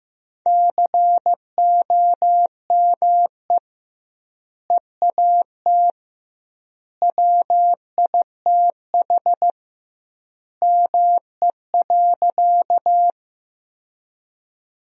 Morse code, heard as a keyed tone, about 16 wpm, 700 Hz. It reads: COME EAT WITH ME.
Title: 2x Word-Spacing - Sentences from Top 300 Words - 15wpm